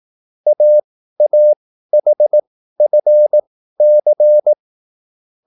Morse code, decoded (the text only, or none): AAHFC